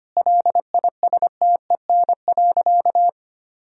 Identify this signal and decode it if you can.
Morse code, heard as a keyed tone, about 25 wpm, 700 Hz. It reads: LISTEN.
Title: Sentences from Top 1000 Words - Review Part 3 - 25wpm